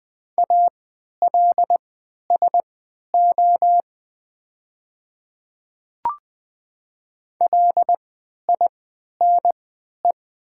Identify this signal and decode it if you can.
Morse code, heard as a keyed tone, about 20 wpm, spaced as for 11 wpm, 700 Hz. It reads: ALSO LINE
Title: 3x Character Spacing - Top 200 Words - 20wpm